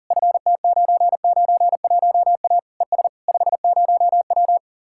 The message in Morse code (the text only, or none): FT991A ES 50W